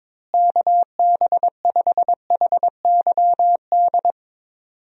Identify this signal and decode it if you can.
Morse code, heard as a keyed tone, about 22 wpm, 700 Hz. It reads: KB5HYD
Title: USA Call Signs - 6 Characters - Repeated 2x - 22wpm